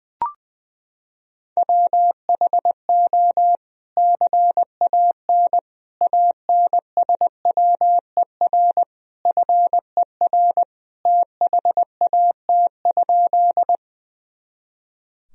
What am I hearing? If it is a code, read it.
Morse code, heard as a keyed tone, about 19 wpm, 700 Hz. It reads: WHO CAN ANSWER FER THAT?
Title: Sentences from Top 300 Words and Abbreviations 20wpm